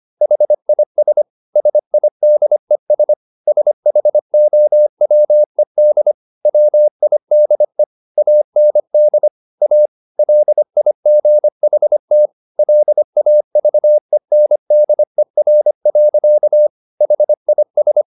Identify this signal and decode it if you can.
Morse code, heard as a keyed tone, about 25 wpm, 600 Hz. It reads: HIS SIDES SHOWED WIDE AND A LIGHT LAVENDER. HIS
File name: the old man and the sea part 1 25wpm